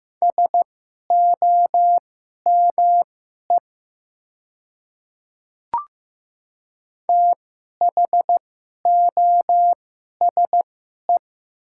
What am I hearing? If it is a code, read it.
Morse code, heard as a keyed tone, about 15 wpm, spaced as for 11 wpm, 700 Hz. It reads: SOME THOSE
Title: 2x Character Spacing - Top 300 Words - 15wpm